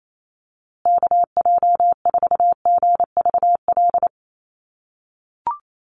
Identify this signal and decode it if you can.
Morse code, heard as a keyed tone, about 28 wpm, 700 Hz. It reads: KJ4GVL
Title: USA Call Signs Repeated 3x 28wpm